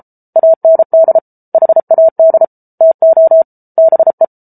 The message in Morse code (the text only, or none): AND HAD TO BE